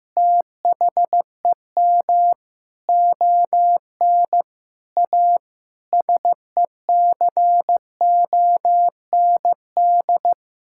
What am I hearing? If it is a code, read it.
Morse code, heard as a keyed tone, about 15 wpm, 700 Hz. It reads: THEM ON A SECOND
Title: Sentences from Top 1000 Words - Review Part 5 - 15wpm